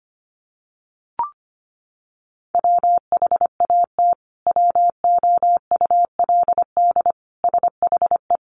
Morse code, heard as a keyed tone, about 25 wpm, 700 Hz. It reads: WHAT WOULD SHE
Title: Sentences from Top 100 Words 25wpm